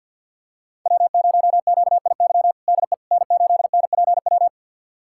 Morse code, heard as a keyed tone, about 51 wpm, 700 Hz. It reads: W0QIY DE N8NPW
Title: USA Callsign DE Callsign - 5 Characters - 50wpm